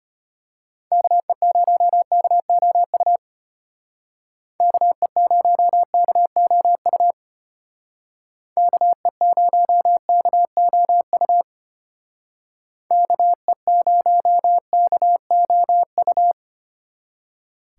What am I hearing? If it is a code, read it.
Morse code, heard as a keyed tone, about 32 wpm, 700 Hz. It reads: KE0KOU KE0KOU KE0KOU KE0KOU